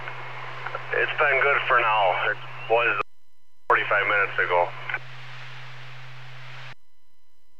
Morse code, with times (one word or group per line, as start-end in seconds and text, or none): none